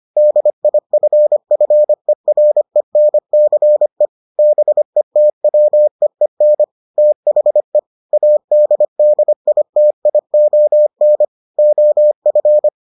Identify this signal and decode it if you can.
Morse code, heard as a keyed tone, about 25 wpm, 600 Hz. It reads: DIFFERENCE BETWEEN THE ADDITION OF